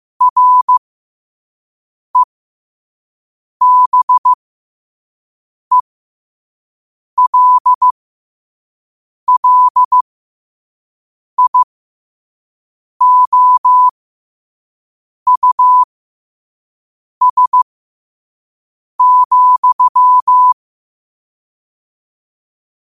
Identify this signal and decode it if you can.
Morse code, heard as a keyed tone, about 15 wpm, spaced as for 5 wpm, 1000 Hz. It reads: REBELLIOUS,
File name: qotdfiles / 5wpm